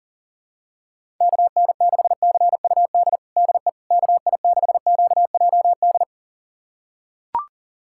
Morse code, heard as a keyed tone, about 40 wpm, 700 Hz. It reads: KN6CUD DE KI6QJD
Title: USA Callsign DE Callsign - 6 Characters - 40wpm